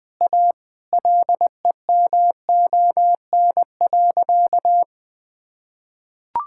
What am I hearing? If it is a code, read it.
Morse code, heard as a keyed tone, about 21 wpm, 700 Hz. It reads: A LEMON.
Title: Sentences with 5-Word Phrases - Spoken Beginning - 20wpm